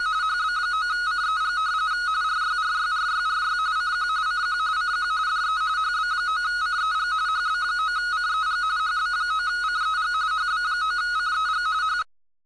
RTTY, about 45.45 baud. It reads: HB9CEG RYRYRYRY CQ CQ CQ DE HB9CEG HB9CEG HB9CEG HB9CEG H
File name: cbfunk / rtty